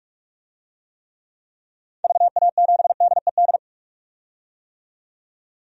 Morse code, heard as a keyed tone, about 46 wpm, 700 Hz. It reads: VA7DED
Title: DXCC-100 Call Signs - 6 Characters - 45wpm